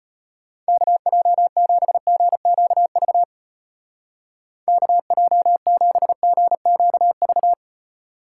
Morse code, heard as a keyed tone, about 32 wpm, 700 Hz. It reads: KJ7GQV KJ7GQV